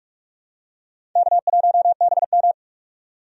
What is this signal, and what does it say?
Morse code, heard as a keyed tone, about 46 wpm, 700 Hz. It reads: K1BM
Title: USA Call Signs - 4 Characters - 45wpm